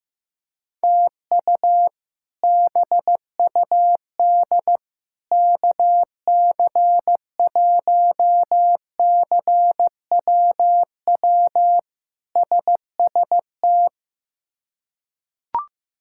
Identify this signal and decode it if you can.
Morse code, heard as a keyed tone, about 15 wpm, 700 Hz. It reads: TU BUD KC1CWW SST
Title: K1USN SST QSO Practice - 15wpm